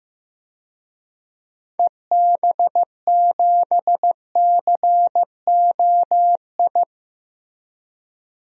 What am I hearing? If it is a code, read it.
Morse code, heard as a keyed tone, about 15 wpm, 690 Hz. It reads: EB7COI
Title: DXCC-100 Call Signs - 6 Characters - 15wpm